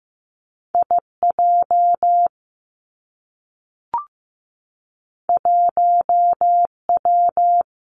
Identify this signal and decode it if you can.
Morse code, heard as a keyed tone, about 15 wpm, 700 Hz. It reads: I J 1 W
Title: Two Challenging Letter-Number Combinations 15wpm